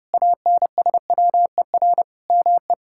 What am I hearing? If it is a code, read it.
Morse code, heard as a keyed tone, about 32 wpm, 700 Hz. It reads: ANSWER ME
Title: Sentences from Top 1000 Words - Review Part 5 - 30wpm